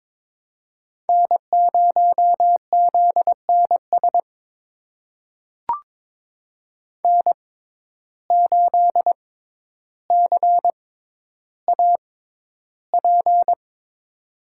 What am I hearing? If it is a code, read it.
Morse code, heard as a keyed tone, about 23 wpm, spaced as for 11 wpm, 700 Hz. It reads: N0ZNS N8CAP
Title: USA Call Signs - 5 Characters - 6x Character Spacing - 22wpm